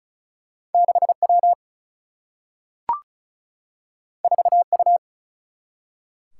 Morse code, heard as a keyed tone, about 36 wpm, 700 Hz. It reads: 6 W 4 U
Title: Single-Letter Single-Number 35wpm